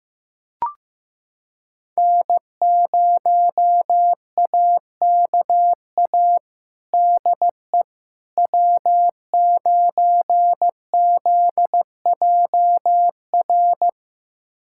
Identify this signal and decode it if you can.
Morse code, heard as a keyed tone, about 15 wpm, 700 Hz. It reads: N0AKA DE W9ZJR
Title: USA Callsign DE Callsign - 5 Characters - 15wpm